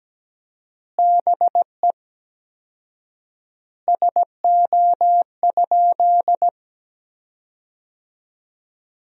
Morse code, heard as a keyed tone, about 17 wpm, 700 Hz. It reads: BE SO?
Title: 4x Word-Spacing - Sentences from Top 300 Words - 17wpm